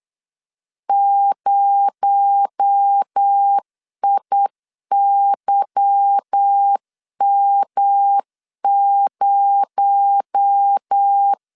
Morse code, heard as a keyed tone, about 8 wpm, 800 Hz. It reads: 0IYM0
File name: LEZIONI / 8 YJ45 / 8 Y 8wpm 3245